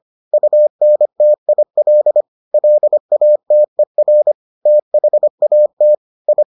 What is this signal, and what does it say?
Morse code, heard as a keyed tone, about 25 wpm, 600 Hz. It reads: UNTIL LATER THAT I